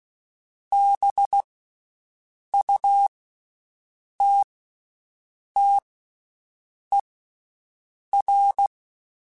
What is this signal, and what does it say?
Morse code, heard as a keyed tone, about 16 wpm, 800 Hz. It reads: B U T T E R
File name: Special INGs / Special INGs-mp3s / VerySlowToFaster Ing @15